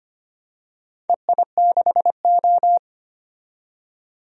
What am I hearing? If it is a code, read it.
Morse code, heard as a keyed tone, about 25 wpm, 700 Hz. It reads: EI6O
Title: DXCC-100 Call Signs - 4 Characters - 25wpm